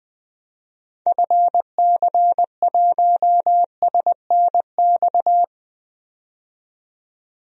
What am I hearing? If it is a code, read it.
Morse code, heard as a keyed tone, about 21 wpm, 700 Hz. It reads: FC1SNX
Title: DXCC-100 Call Signs - 6 Characters - 20wpm